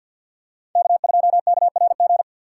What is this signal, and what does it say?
Morse code, heard as a keyed tone, about 49 wpm, 700 Hz. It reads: K2KRG
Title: USA Call Signs - 5 Characters - 6x Character Spacing - 50wpm